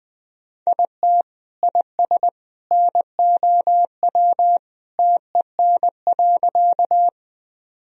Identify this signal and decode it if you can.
Morse code, heard as a keyed tone, about 21 wpm, 700 Hz. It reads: IT IS NOW TEN.
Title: Sentences from Top 1000 Words - Review Part 2 - 20wpm